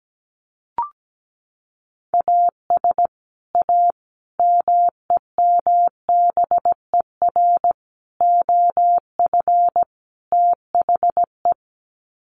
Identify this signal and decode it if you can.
Morse code, heard as a keyed tone, about 17 wpm, 700 Hz. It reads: AS A MEMBER OF THE
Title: Sets of 5 Words 17wpm